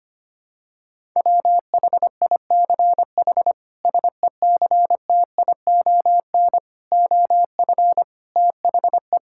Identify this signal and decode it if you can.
Morse code, heard as a keyed tone, about 25 wpm, 700 Hz. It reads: WHICH SECTION OF THE
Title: Sentences from Top 1000 Words - Review Part 5 - 25wpm